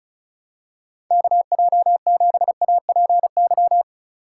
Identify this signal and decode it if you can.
Morse code, heard as a keyed tone, about 36 wpm, 700 Hz. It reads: KJ7APY
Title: USA Call Signs - 6 Characters - 3x Character Spacing - 35wpm